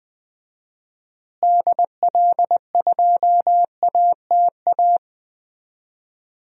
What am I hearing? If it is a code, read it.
Morse code, heard as a keyed tone, about 20 wpm, 700 Hz. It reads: DL2ATA
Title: DXCC-100 Call Signs - 6 Characters - Repeated 2x - 20wpm